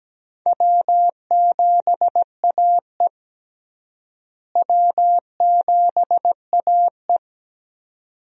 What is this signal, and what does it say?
Morse code, heard as a keyed tone, about 17 wpm, 700 Hz. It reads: W7AE W7AE
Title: USA Call Signs - 4 Characters - Repeated 2x - 17wpm